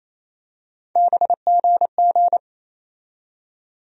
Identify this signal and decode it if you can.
Morse code, heard as a keyed tone, about 28 wpm, 700 Hz. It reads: BGG
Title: USA Call Sign Postfixes - 3 Letters 28wpm